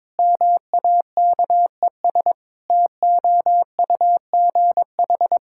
Morse code, heard as a keyed tone, about 22 wpm, 700 Hz. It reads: MAKES TOUGH
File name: Difficult Sentences with Hints - 22wpm